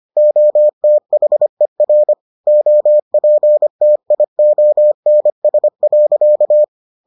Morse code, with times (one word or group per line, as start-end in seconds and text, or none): 0.2-2.1 OTHER
2.5-6.6 OPTIONS.